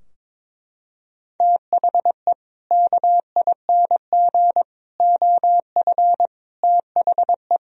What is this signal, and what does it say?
Morse code, heard as a keyed tone, about 22 wpm, 700 Hz. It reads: THE KING OF THE
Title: Sentences from Top 1000 Words - Review Part 2 - 22wpm